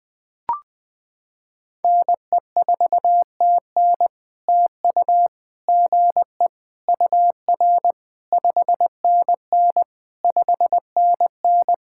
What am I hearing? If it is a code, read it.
Morse code, heard as a keyed tone, about 20 wpm, 700 Hz. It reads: NE4TN TU GE UR 5NN 5NN
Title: POTA QSO Practice - 20wpm